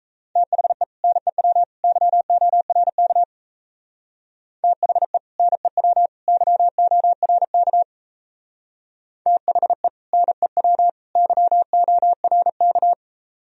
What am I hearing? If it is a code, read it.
Morse code, heard as a keyed tone, about 35 wpm, 700 Hz. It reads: THE NEW YORK THE NEW YORK THE NEW YORK